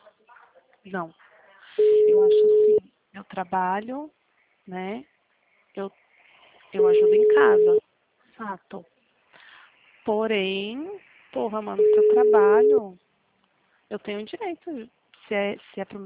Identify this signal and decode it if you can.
Ringback tone (Europe, 425 Hz), with speech.